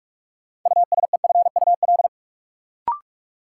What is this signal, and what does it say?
Morse code, heard as a keyed tone, about 46 wpm, 700 Hz. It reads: USEFUL
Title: Sentences from Top 700 Words - New Word Drill - 45wpm